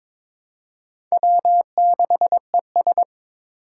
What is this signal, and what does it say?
Morse code, heard as a keyed tone, about 21 wpm, 700 Hz. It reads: W6ES
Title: Four Challenging Letter-Number Combinations 22wpm